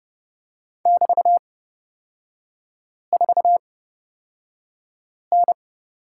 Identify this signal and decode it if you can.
Morse code, heard as a keyed tone, about 31 wpm, spaced as for 5 wpm, 690 Hz. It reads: <BT>4N
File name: Lesson 37e - Letter x - Three Characters 30wpm